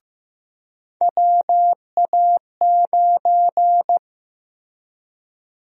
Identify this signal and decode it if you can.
Morse code, heard as a keyed tone, about 15 wpm, 700 Hz. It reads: WA9
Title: USA Call Sign Prefixes - 15wpm